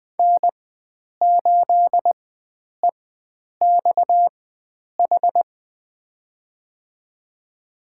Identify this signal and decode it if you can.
Morse code, heard as a keyed tone, about 21 wpm, spaced as for 9 wpm, 700 Hz. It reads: N8EXH
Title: USA Call Signs - 4x Character Spacing - 20wpm